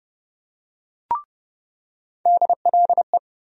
Morse code, heard as a keyed tone, about 30 wpm, 700 Hz. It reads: DLE
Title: USA Call Sign Postfixes - 3 Letters 30wpm